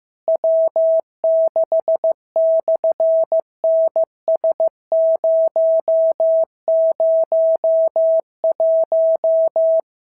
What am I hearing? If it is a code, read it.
Morse code, heard as a keyed tone, about 15 wpm, 650 Hz. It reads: W6/NS001
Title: SOTA QSO Practice 15wpm